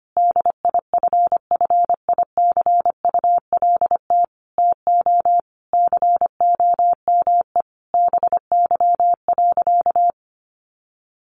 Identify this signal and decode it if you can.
Morse code, heard as a keyed tone, about 25 wpm, 700 Hz. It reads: DIFFICULT TO COME BY.
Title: Sentences from Top 1500 Words 25wpm